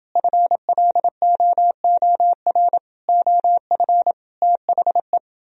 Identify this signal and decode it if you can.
Morse code, heard as a keyed tone, about 29 wpm, 700 Hz. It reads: FLOOR OF THE